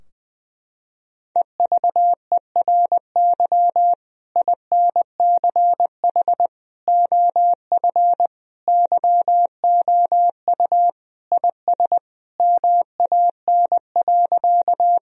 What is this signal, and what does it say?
Morse code, heard as a keyed tone, about 20 wpm, 700 Hz. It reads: EVERY INCH OF YOU IS MAN.
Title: Sentences from Top 1000 Words - Review Part 3 - 20wpm